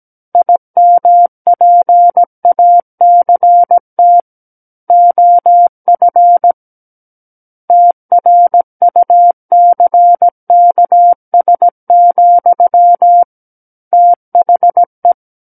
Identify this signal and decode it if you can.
Morse code, heard as a keyed tone, about 17 wpm, 700 Hz. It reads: IMPACT OF TRUCKS, THE
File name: Book One / 08-Chapter 8 - 17 WPM